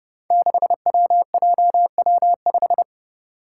Morse code, heard as a keyed tone, about 30 wpm, 700 Hz. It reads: 6WJW5
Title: Five Challenging Letter-Number Combinations 30wpm